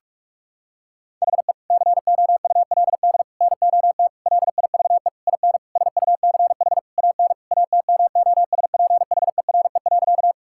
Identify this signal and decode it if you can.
Morse code, heard as a keyed tone, about 41 wpm, 700 Hz. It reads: HE COULD NOT LIVE IN SUCH AN ATMOSPHERE.
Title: Sentences from Top 1600 Words 45wpm